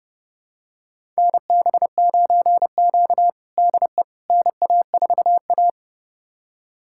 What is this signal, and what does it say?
Morse code, heard as a keyed tone, about 30 wpm, 700 Hz. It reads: NB9Q DE NA4A
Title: USA Callsign DE Callsign - 4 Characters - 30wpm